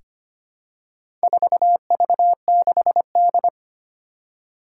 Morse code, heard as a keyed tone, about 25 wpm, 700 Hz. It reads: 4V6D
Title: Four Challenging Letter-Number Combinations 25wpm